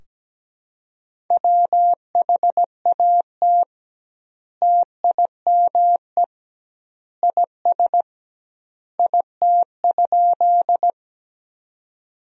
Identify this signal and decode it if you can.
Morse code, heard as a keyed tone, about 17 wpm, 700 Hz. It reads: WHAT TIME IS IT?
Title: 2x Word-Spacing - Sentences from Top 100 Words - 17wpm